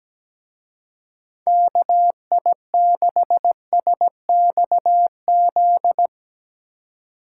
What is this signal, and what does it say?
Morse code, heard as a keyed tone, about 18 wpm, 700 Hz. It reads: KI6SXZ
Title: Speed Racing - USA Call Signs 17wpm